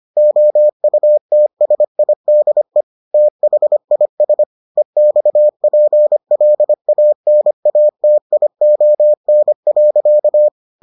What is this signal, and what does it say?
Morse code, heard as a keyed tone, about 25 wpm, 600 Hz. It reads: OUTSIDE THIS EXPLANATION.